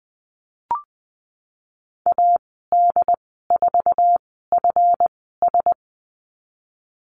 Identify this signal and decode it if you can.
Morse code, heard as a keyed tone, about 20 wpm, spaced as for 15 wpm, 700 Hz. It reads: AD4FS
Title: 161 Lesson 26g - Letter g - Call Signs 20wpm